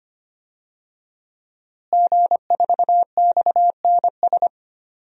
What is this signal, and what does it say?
Morse code, heard as a keyed tone, about 25 wpm, 700 Hz. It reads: G4XNS
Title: DXCC-100 Call Signs - 5 Characters - Repeated 2x - 25wpm